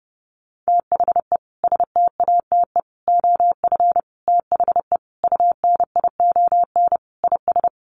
Morse code, heard as a keyed tone, about 30 wpm, 700 Hz. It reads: THE STATE OF THE UNION IS
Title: Sentences from Top 1400 Words 30wpm